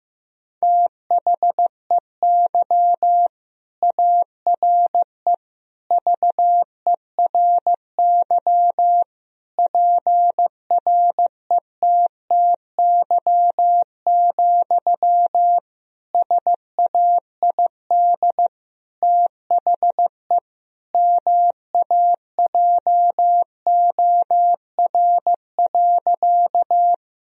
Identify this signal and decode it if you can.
Morse code, heard as a keyed tone, about 15 wpm, 700 Hz. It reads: THEY ARE VERY PRETTY, SAID THE MAJOR.
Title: Sentences from Top 1000 Words - Review Part 4 - 15wpm